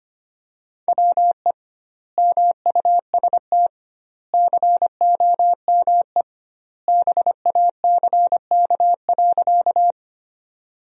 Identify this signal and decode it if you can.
Morse code, heard as a keyed tone, about 25 wpm, 700 Hz. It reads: WE MUST COME BACK.
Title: 2x Word-Spacing - Sentences from Top 200 Words - 25wpm